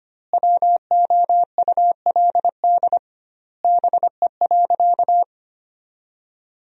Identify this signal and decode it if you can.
Morse code, heard as a keyed tone, about 25 wpm, 700 Hz. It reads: WOULD BE.
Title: 2x Word-Spacing - Sentences from Top 200 Words - 25wpm